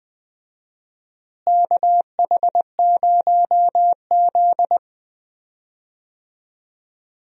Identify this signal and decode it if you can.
Morse code, heard as a keyed tone, about 20 wpm, 700 Hz. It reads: KH0Z
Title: USA Call Signs - 4 Characters - Repeated 3x - 20wpm